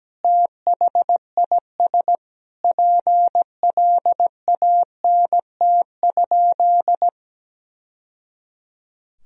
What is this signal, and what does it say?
Morse code, heard as a keyed tone, about 17 wpm, 700 Hz. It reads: THIS PLANT?
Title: Sentences from Top 1000 Words - Review Part 1 - 17wpm